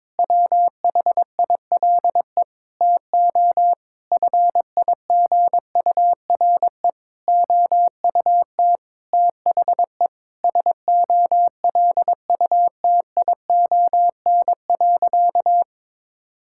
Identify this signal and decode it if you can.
Morse code, heard as a keyed tone, about 22 wpm, 700 Hz. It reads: WHILE TO FIGURE OUT THE SOLUTION.
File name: practice / 22wpm / Sentences with 8-Word Phrases - 22wpm